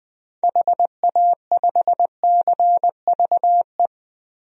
Morse code, heard as a keyed tone, about 20 wpm, 700 Hz. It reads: HA5CVE
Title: DXCC-100 Call Signs - 6 Characters - Repeated 2x - 20wpm